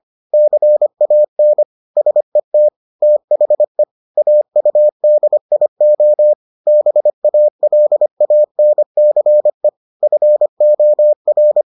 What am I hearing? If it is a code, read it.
Morse code, heard as a keyed tone, about 25 wpm, 600 Hz. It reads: CAN SET THE AUDIO BALANCE FOR